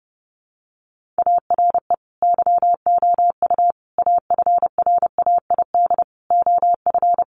Morse code, heard as a keyed tone, about 30 wpm, 700 Hz. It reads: ARE YOU AFRAID OF
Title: Sentences from Top 1500 Words 30wpm